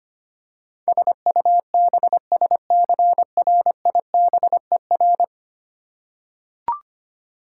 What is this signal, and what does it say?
Morse code, heard as a keyed tone, about 25 wpm, 700 Hz. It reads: SUBSCRIBER
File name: Ten Letter Words - 25wpm